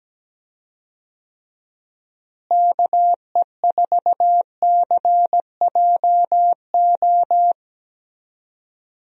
Morse code, heard as a keyed tone, about 17 wpm, 700 Hz. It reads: KE4CJO
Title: USA Call Signs - 6 Characters - Repeated 2x - 17wpm